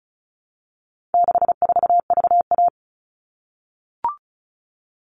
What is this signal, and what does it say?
Morse code, heard as a keyed tone, about 34 wpm, 700 Hz. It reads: 64VA
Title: All Cut Numbers - 4-Digit 35wpm